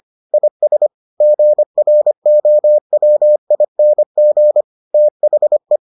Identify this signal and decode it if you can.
Morse code, heard as a keyed tone, about 25 wpm, 600 Hz. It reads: IS GROWING THE